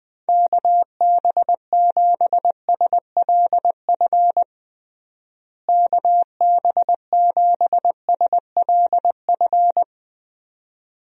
Morse code, heard as a keyed tone, about 21 wpm, 700 Hz. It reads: KB7SLF KB7SLF
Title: USA Call Signs - 6 Characters - Repeated 2x - 20wpm